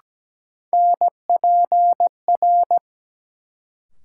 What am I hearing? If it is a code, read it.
Morse code, heard as a keyed tone, about 17 wpm, 700 Hz. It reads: NPR
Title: Sets of 5 Words - New Word Drill - 17wpm